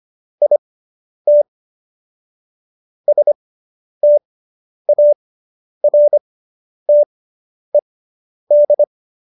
Morse code, heard as a keyed tone, about 25 wpm, spaced as for 10 wpm, 590 Hz. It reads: IT STARTED